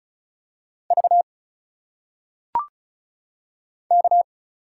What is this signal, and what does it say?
Morse code, heard as a keyed tone, about 33 wpm, 700 Hz. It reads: V K
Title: Single Letters - Phonetic Pronunciation 35wpm